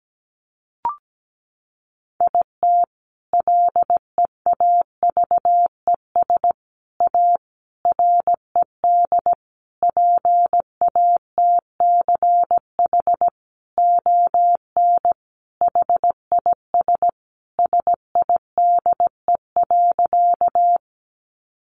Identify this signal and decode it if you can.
Morse code, heard as a keyed tone, about 17 wpm, 700 Hz. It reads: IT LEAVES A RED PATCH ON HIS SIDE.
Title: A Sound Of Thunder 17wpm